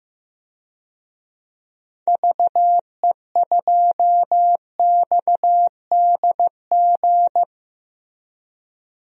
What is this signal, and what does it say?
Morse code, heard as a keyed tone, about 16 wpm, 700 Hz. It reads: VE2XDG
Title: DXCC-100 Call Signs - 6 Characters - Repeated 3x - 15wpm